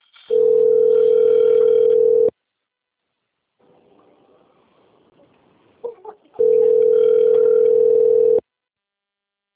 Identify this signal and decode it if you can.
Ringback tone (North America), 440 + 480 Hz.